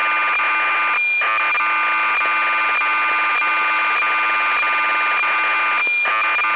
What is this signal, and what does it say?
Cassette data (Kansas City type).